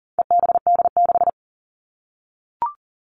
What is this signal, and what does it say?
Morse code, heard as a keyed tone, about 40 wpm, 700 Hz. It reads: EBD6
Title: All Cut Numbers - 4-Digit 40wpm